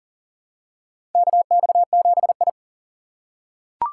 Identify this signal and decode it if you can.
Morse code, heard as a keyed tone, about 40 wpm, 700 Hz. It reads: KX7I
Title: USA Call Signs - 4 Characters - 4x Character Spacing - 40wpm